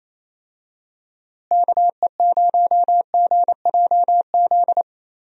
Morse code, heard as a keyed tone, about 28 wpm, 700 Hz. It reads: KE0GJZ
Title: USA Call Signs - 6 Characters - Repeated 2x - 28wpm